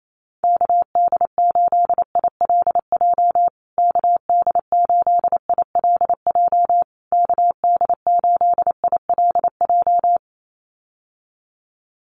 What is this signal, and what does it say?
Morse code, heard as a keyed tone, about 28 wpm, 700 Hz. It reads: KD8ILJ KD8ILJ KD8ILJ
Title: USA Call Signs Repeated 3x 28wpm